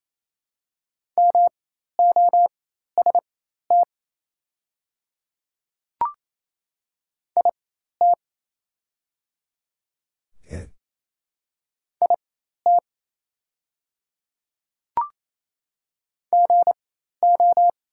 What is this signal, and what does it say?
Morse code, heard as a keyed tone, about 27 wpm, spaced as for 13 wpm, 690 Hz. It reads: MOST IT IT GO
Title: 4x Character Spacing - Top 100 Words - 28wpm